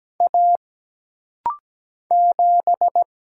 Morse code, heard as a keyed tone, about 17 wpm, 700 Hz. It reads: A 7
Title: Single Letter-Number - Rapid-Fire 17wpm